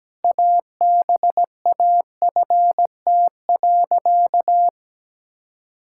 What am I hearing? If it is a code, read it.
Morse code, heard as a keyed tone, about 17 wpm, 700 Hz. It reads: ABAFT.
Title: Pangram Challenge v2 - 17wpm